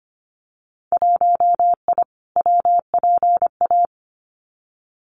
Morse code, heard as a keyed tone, about 25 wpm, 700 Hz. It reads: 1I WPA
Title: Winter Field Day Exchanges Encore 25wpm